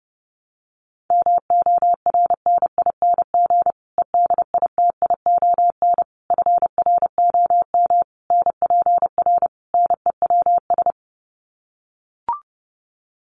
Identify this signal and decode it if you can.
Morse code, heard as a keyed tone, about 30 wpm, 700 Hz. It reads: MORNING EDITION FROM NPR NEWS
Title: Sets of 5 Words Encore 30wpm